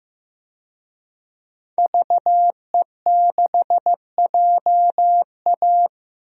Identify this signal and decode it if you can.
Morse code, heard as a keyed tone, about 15 wpm, 700 Hz. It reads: VE6JA
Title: DXCC-100 Call Signs - 5 Characters - 15wpm